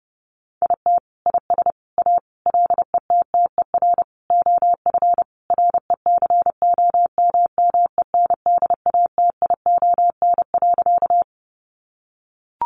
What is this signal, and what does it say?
Morse code, heard as a keyed tone, about 28 wpm, 700 Hz. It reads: IT IS A LETTER OF RECOMMENDATION.
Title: Sentences from Top 1400 Words 30wpm